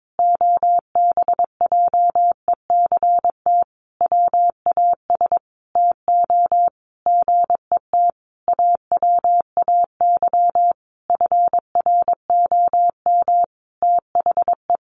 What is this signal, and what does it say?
Morse code, heard as a keyed tone, about 22 wpm, 700 Hz. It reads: OBJECT WAS TO GET AWAY FROM THE
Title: Sentences from Top 1500 Words 22wpm